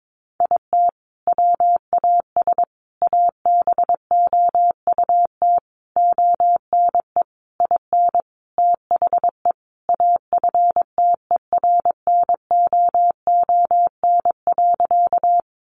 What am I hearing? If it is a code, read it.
Morse code, heard as a keyed tone, about 22 wpm, 700 Hz. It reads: IT WAS ABOUT ONE IN THE AFTERNOON.
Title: Sentences from Top 1500 Words 22wpm